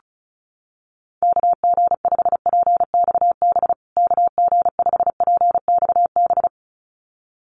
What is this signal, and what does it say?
Morse code, heard as a keyed tone, about 34 wpm, 700 Hz. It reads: KG5PXB KG5PXB
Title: USA Call Signs Repeated 2x 35wpm